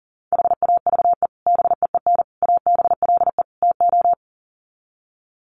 Morse code, heard as a keyed tone, about 40 wpm, 700 Hz. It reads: HAVE BEEN ABLE TO
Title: Sets of 4 Words 40wpm